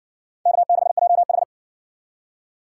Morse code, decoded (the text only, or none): K6JH